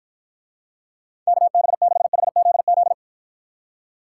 Morse code, heard as a keyed tone, about 54 wpm, 700 Hz. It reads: KB6HZZ